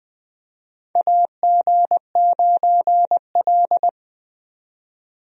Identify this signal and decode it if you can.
Morse code, heard as a keyed tone, about 20 wpm, 700 Hz. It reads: AG9L
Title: USA Call Signs - 4 Characters - 20wpm